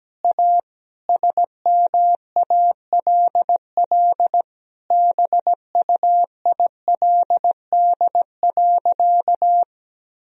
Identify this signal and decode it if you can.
Morse code, heard as a keyed tone, about 17 wpm, 700 Hz. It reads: A SMALL BUILD.
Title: Sentences from Top 1000 Words - Review Part 1 - 17wpm